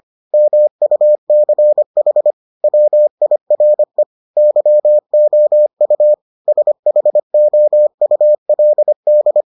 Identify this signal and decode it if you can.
Morse code, heard as a keyed tone, about 25 wpm, 600 Hz. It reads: MUCH WIRE YOU SHOULD